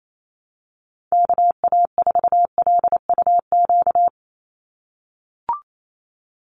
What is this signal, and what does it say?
Morse code, heard as a keyed tone, about 28 wpm, 700 Hz. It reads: KA4LUQ
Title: USA Call Signs 28wpm